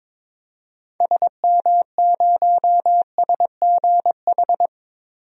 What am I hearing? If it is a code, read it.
Morse code, heard as a keyed tone, about 21 wpm, 700 Hz. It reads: SM0SGH
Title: DXCC-100 Call Signs - 6 Characters - Repeated 2x - 22wpm